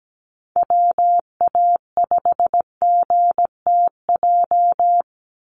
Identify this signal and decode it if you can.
Morse code, heard as a keyed tone, about 17 wpm, 700 Hz. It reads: WA5GTJ
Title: USA Call Signs 17wpm